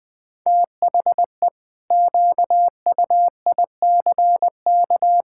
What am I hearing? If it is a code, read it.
Morse code, heard as a keyed tone, about 20 wpm, 700 Hz. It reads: THE QUICK
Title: Sentences with 8-Word Phrases - 20wpm